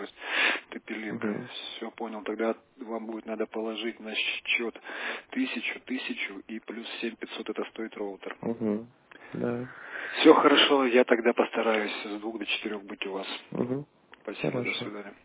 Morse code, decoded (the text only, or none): none